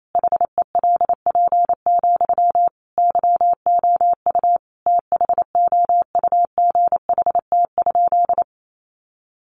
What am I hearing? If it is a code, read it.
Morse code, heard as a keyed tone, about 28 wpm, 700 Hz. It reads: HELP, YOU THOUGHT?
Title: Sentences from Top 300 Words 28wpm